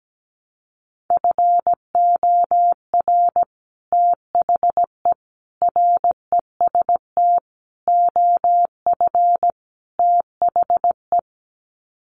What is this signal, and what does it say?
Morse code, heard as a keyed tone, about 17 wpm, 700 Hz. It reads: FOR THE REST OF THE
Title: Sets of 5 Words 17wpm